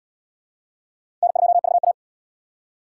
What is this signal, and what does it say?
Morse code, heard as a keyed tone, about 67 wpm, 700 Hz. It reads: N2VA